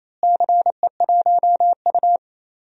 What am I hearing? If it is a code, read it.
Morse code, heard as a keyed tone, about 28 wpm, 700 Hz. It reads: CE1U
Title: DXCC-100 Call Signs - 4 Characters - Repeated 3x - 28wpm